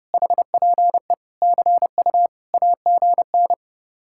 Morse code, heard as a keyed tone, about 38 wpm, 700 Hz. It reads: HPE CU AGN